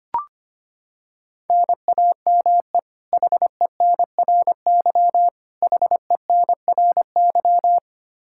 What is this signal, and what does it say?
Morse code, heard as a keyed tone, about 25 wpm, 700 Hz. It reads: NAME HENRY HENRY
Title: Practice QSO - Exchange 1 - Beginning - 25wpm